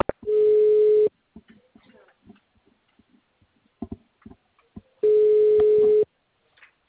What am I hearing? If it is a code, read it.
Ringback tone (Europe, 425 Hz).